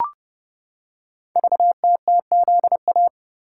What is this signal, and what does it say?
Morse code, heard as a keyed tone, about 30 wpm, 700 Hz. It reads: VTTZA
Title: Code Groups - 5 Letters - 30wpm